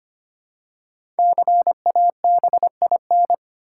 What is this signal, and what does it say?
Morse code, heard as a keyed tone, about 25 wpm, 700 Hz. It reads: CABIN
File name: 5 Letter Words - First 3 Letters Given - 25wpm